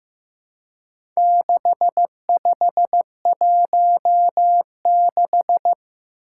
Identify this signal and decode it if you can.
Morse code, heard as a keyed tone, about 15 wpm, 700 Hz. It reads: 6516
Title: Four Challenging Letter-Number Combinations 15wpm